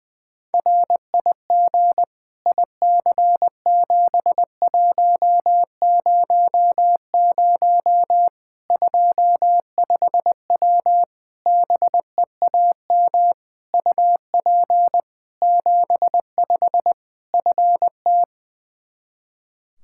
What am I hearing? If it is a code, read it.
Morse code, heard as a keyed tone, about 20 wpm, 700 Hz. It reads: RIG IC7100 25W BEAM UP 75 FT
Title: Practice QSO - Exchange 2 - Info - 20wpm